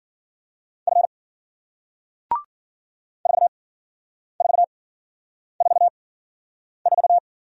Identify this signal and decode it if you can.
Morse code, heard as a keyed tone, about 51 wpm, spaced as for 9 wpm, 700 Hz. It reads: F 4444